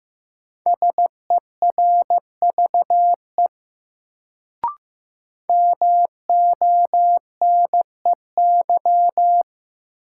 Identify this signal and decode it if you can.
Morse code, heard as a keyed tone, about 15 wpm, 700 Hz. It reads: SERVE MONEY
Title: Sentences from Top 400 Words - New Word Drill - 15wpm